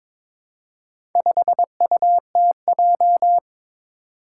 Morse code, heard as a keyed tone, about 22 wpm, 700 Hz. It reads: 5UTJ
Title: Four Challenging Letter-Number Combinations 22wpm